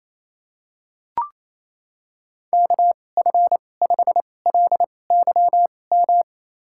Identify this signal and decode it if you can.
Morse code, heard as a keyed tone, about 28 wpm, spaced as for 20 wpm, 700 Hz. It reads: KF5LYM
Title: USA Call Signs - 2x Character Spacing - 28wpm